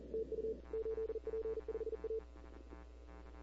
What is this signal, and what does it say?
Morse code, heard as a keyed tone, about 40 wpm, 440 Hz. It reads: TU 8P5A
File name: CW / 2005-k5tr arrl cw dx -15:00